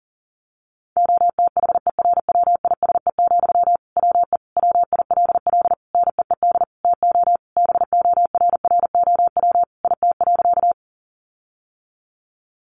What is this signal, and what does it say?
Morse code, heard as a keyed tone, about 40 wpm, 700 Hz. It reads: OTHERWISE, WE WILL NEED TO BORROW IT.
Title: Sentences from Top 1500 Words 40wpm